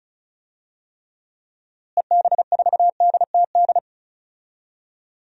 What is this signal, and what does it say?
Morse code, heard as a keyed tone, about 35 wpm, 700 Hz. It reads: EB4DTD
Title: DXCC-100 Call Signs - 6 Characters - Repeated 3x - 35wpm